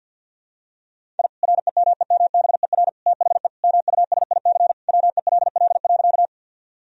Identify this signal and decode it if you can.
Morse code, heard as a keyed tone, about 51 wpm, 700 Hz. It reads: I REMEMBER THE MUSIC WELL.
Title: Sentences from Top 1000 Words - Review Part 1 - 50wpm